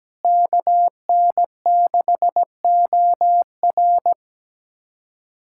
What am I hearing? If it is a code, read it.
Morse code, heard as a keyed tone, about 17 wpm, 700 Hz. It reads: KN6OR
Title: USA Callsign DE Callsign 17wpm